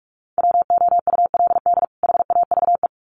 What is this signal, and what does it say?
Morse code, heard as a keyed tone, about 44 wpm, 700 Hz. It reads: WOULD HAVE
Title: Sets of 2 Words 45wpm